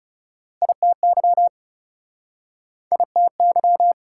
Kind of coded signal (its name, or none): Morse code